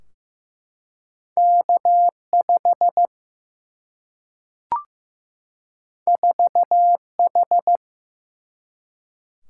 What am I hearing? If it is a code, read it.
Morse code, heard as a keyed tone, about 15 wpm, 700 Hz. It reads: K5 4H
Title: Single-Letter Single-Number 15wpm